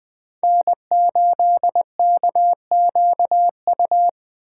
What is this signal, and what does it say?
Morse code, heard as a keyed tone, about 20 wpm, 700 Hz. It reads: N8KQU
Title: USA Call Signs - 5 Characters - 2x Character Spacing - 20wpm